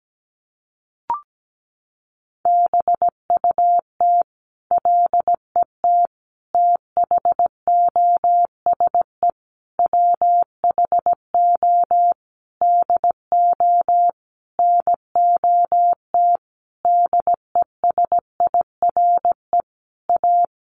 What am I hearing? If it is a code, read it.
Morse code, heard as a keyed tone, about 17 wpm, 700 Hz. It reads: BUT LET THOSE WHO DO NOT DESIRE A
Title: Sentences from Top 1500 Words 17wpm